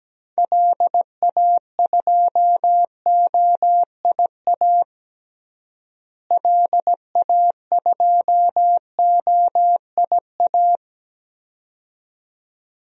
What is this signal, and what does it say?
Morse code, heard as a keyed tone, about 17 wpm, 700 Hz. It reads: LA2OIA LA2OIA
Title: DXCC-100 Call Signs - 6 Characters - Repeated 2x - 17wpm